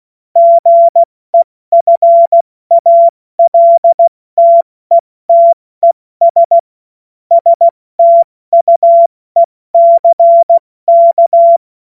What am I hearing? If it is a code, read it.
Morse code, heard as a keyed tone, about 16 wpm, 680 Hz. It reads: GEFALTETES STUECK
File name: Morsekurs / Witze / Witz 14.November 2023 70 F80